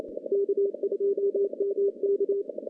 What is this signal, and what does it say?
Morse code, heard as a keyed tone, about 26 wpm, 400 Hz. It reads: K2MK
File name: contests / CW-SS-k5tr- -1300